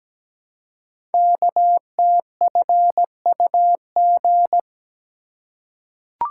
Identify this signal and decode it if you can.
Morse code, heard as a keyed tone, about 17 wpm, 700 Hz. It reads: KTFUG
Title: Code Groups - 5 Letters - 17wpm